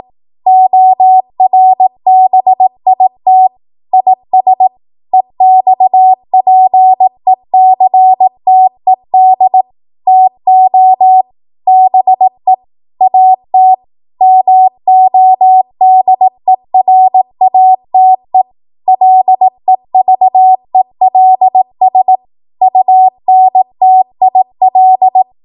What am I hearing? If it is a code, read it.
Morse code, heard as a keyed tone, about 18 wpm, 750 Hz. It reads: ORBIT IS EXPECTED TO BE AT MODERATE LEVELS UNTIL
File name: Morse / 18WPM